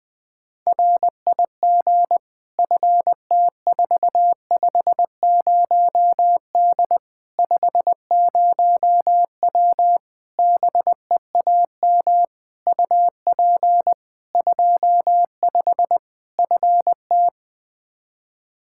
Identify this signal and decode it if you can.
Morse code, heard as a keyed tone, about 20 wpm, 700 Hz. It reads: RIG FT450D 50W BEAM UP 25 FT
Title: Practice QSO - Exchange 2 - Info - 20wpm